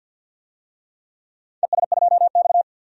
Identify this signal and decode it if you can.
Morse code, heard as a keyed tone, about 48 wpm, 700 Hz. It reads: ES2X
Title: DXCC-100 Call Signs - 4 Characters - 50wpm